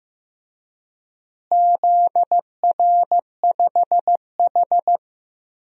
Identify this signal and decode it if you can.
Morse code, heard as a keyed tone, about 15 wpm, 700 Hz. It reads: ZR5H
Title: DXCC-100 Call Signs - 4 Characters - Repeated 2x - 15wpm